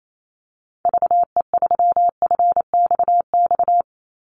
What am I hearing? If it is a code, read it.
Morse code, heard as a keyed tone, about 28 wpm, 700 Hz. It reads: VE3FXX
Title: Call Signs - 28wpm